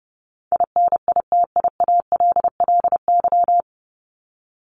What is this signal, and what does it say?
Morse code, heard as a keyed tone, about 30 wpm, 700 Hz. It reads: INITIALLY
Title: Top 1251-1500 Words 30wpm